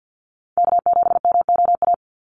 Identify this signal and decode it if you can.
Morse code, heard as a keyed tone, about 50 wpm, 700 Hz. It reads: K7MOA
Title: Call Signs - 50wpm